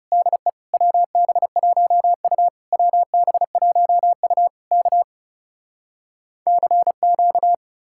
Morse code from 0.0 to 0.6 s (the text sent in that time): DE